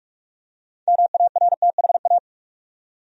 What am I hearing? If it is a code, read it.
Morse code, heard as a keyed tone, about 46 wpm, 700 Hz. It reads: MARTHA
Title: USA Common Names - 45wpm